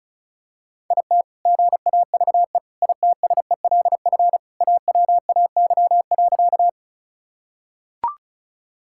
Morse code, heard as a keyed tone, about 36 wpm, 700 Hz. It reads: IT GAVE ITSELF AWAY.
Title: Sentences from Top 1000 Words - Review Part 5 - 35wpm